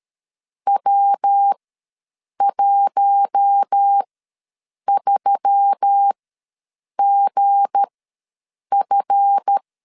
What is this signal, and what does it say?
Morse code, heard as a keyed tone, about 13 wpm, spaced as for 7 wpm, 800 Hz. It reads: W13GF
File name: LEZIONI / 7 GF9Z / 7 GF ws 12wpm 3140